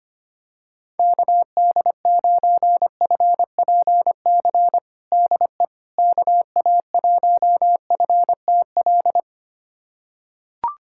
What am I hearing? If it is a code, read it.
Morse code, heard as a keyed tone, about 25 wpm, 700 Hz. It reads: KD9FPC DE KA1FTL
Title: USA Callsign DE Callsign - 6 Characters - 25wpm